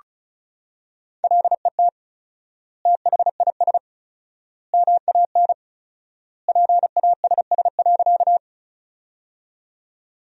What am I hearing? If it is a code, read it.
Morse code, heard as a keyed tone, about 36 wpm, 700 Hz. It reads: LET THIS MAN PASS.
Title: 4x Word-Spacing - Sentences from Top 400 Words - 35wpm